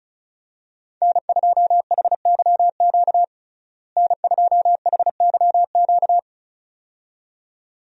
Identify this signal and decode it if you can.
Morse code, heard as a keyed tone, about 34 wpm, 700 Hz. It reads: N2HYQ N2HYQ
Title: USA Call Signs - 5 Characters - Repeated 2x - 35wpm